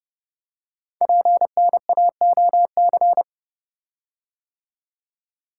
Morse code, heard as a keyed tone, about 30 wpm, 700 Hz. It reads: PNAOC
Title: Code Groups - 5 Letters - 30wpm